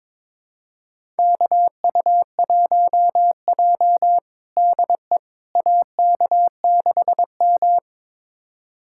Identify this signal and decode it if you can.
Morse code, heard as a keyed tone, about 22 wpm, 700 Hz. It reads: KU1J DE AK6M
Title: USA Callsign DE Callsign - 4 Characters - 22wpm